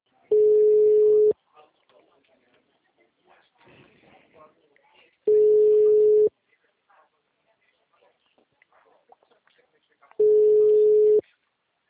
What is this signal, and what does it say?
Ringback tone (Europe, 425 Hz), with speech.